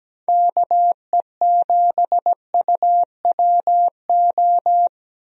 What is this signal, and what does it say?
Morse code, heard as a keyed tone, about 17 wpm, 700 Hz. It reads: KE7UWO
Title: USA Call Signs - 6 Characters - Repeated 3x - 17wpm